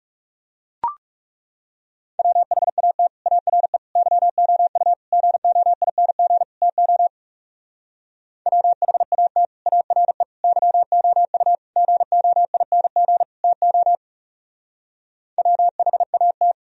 Morse code, heard as a keyed tone, about 37 wpm, 700 Hz. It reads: WHAT ARE YOU GOING TO WHAT ARE YOU GOING TO WHAT
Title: Speed-Racing - Sets of 5 Words 30wpm